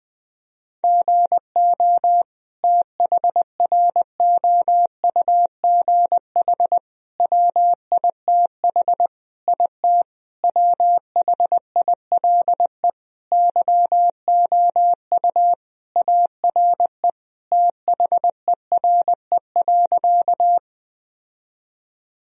Morse code, heard as a keyed tone, about 20 wpm, 700 Hz. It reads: GO THROUGH WITH IT WHILE YOU ARE THERE.
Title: Sentences from Top 1000 Words - Review Part 1 - 20wpm